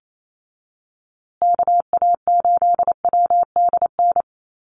Morse code, heard as a keyed tone, about 28 wpm, 700 Hz. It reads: KA8WDN
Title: USA Call Signs Repeated 3x 28wpm